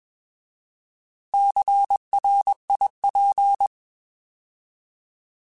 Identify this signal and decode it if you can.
Morse code, heard as a keyed tone, about 21 wpm, 800 Hz. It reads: CRIP